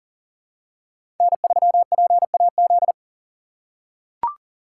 Morse code, heard as a keyed tone, about 40 wpm, 700 Hz. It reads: N3PAZ
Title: USA Call Signs - 5 Characters - 2x Character Spacing - 40wpm